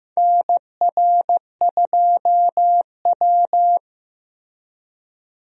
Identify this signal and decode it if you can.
Morse code, heard as a keyed tone, about 15 wpm, 700 Hz. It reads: NR2W
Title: USA Call Signs - 4 Characters - Repeated 3x - 15wpm